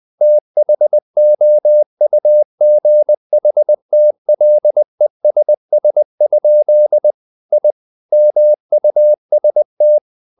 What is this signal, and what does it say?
Morse code, heard as a keyed tone, about 20 wpm, 600 Hz. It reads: THOUGHTLESS? I MUST
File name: the old man and the sea / the old man and the sea part 1 20wpm